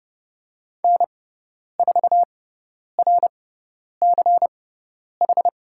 Morse code, heard as a keyed tone, about 30 wpm, spaced as for 10 wpm, 700 Hz. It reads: N4RCH
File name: course / 30:10wpm / 102 Lesson 17g - Letter u - Call Signs 30wpm